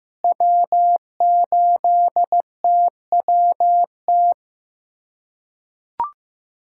Morse code, heard as a keyed tone, about 15 wpm, 700 Hz. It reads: W8TWT
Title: USA Callsign DE Callsign - 5 Characters - 15wpm